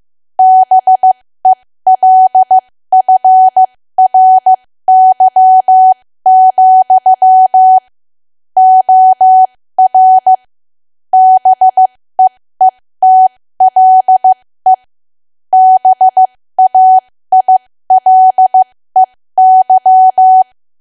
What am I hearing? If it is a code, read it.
Morse code, heard as a keyed tone, about 15 wpm, spaced as for 13 wpm, 750 Hz. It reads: BELFRY, OR BEETLE BAILEY
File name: cwa / W1AW-Level2 / 13WPM